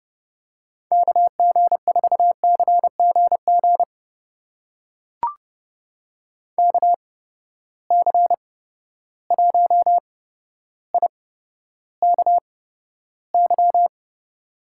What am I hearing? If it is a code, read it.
Morse code, heard as a keyed tone, about 31 wpm, spaced as for 13 wpm, 700 Hz. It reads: KG4CGG KC1IKY
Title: USA Call Signs - 6 Characters - 8x Character Spacing - 30wpm